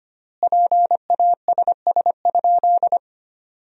Morse code, heard as a keyed tone, about 25 wpm, 700 Hz. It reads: PASS?
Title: 4x Word-Spacing - Sentences from Top 400 Words - 25wpm